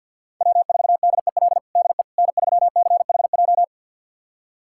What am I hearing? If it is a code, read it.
Morse code, heard as a keyed tone, about 51 wpm, 700 Hz. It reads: W4DEL DE N3CHJ
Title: USA Callsign DE Callsign - 5 Characters - 50wpm